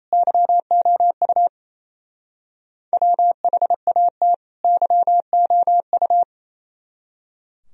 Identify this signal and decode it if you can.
Morse code, heard as a keyed tone, about 33 wpm, 700 Hz. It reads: YOU WHAT YOU
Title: Speed-Racing - Sets of 2 Words 28wpm